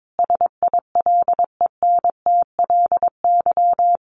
Morse code, heard as a keyed tone, about 22 wpm, 700 Hz. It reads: SILENTLY
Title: A Sound Of Thunder 22wpm